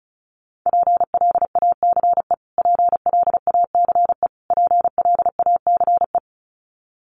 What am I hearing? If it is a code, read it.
Morse code, heard as a keyed tone, about 34 wpm, 700 Hz. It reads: PLACE PLACE PLACE
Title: Top 200 Words Repeated 35wpm